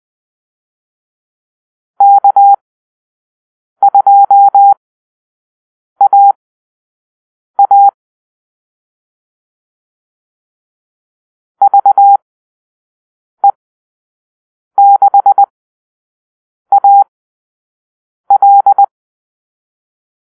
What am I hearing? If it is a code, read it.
Morse code, heard as a keyed tone, about 20 wpm, spaced as for 6 wpm, 800 Hz. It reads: K2AA VE6AL
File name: arrl / Call-Sign Drill-5